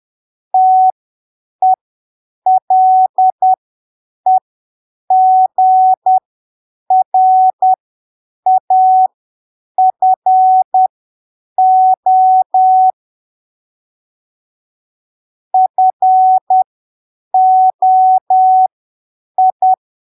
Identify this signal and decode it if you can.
Morse code, heard as a keyed tone, about 10 wpm, spaced as for 7 wpm, 750 Hz. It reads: TELEGRAFO FOI